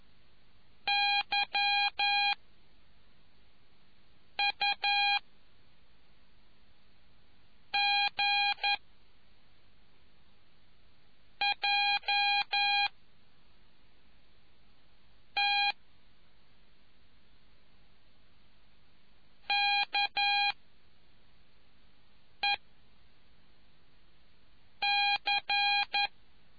Morse code, heard as a keyed tone, about 11 wpm, 2450 Hz. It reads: Y U G J T K E C